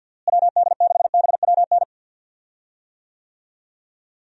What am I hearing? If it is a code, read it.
Morse code, heard as a keyed tone, about 51 wpm, 700 Hz. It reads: WD6BWN
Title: USA Call Signs - 6 Characters - Repeated 2x - 50wpm